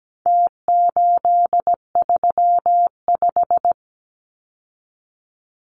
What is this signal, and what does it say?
Morse code, heard as a keyed tone, about 17 wpm, 700 Hz. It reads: T835
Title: Numbers - Contest Style - 4-Digit 17wpm